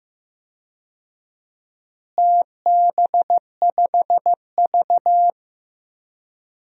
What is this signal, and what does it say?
Morse code, heard as a keyed tone, about 15 wpm, 700 Hz. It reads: TB5V
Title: DXCC-100 Call Signs - 4 Characters - Repeated 2x - 15wpm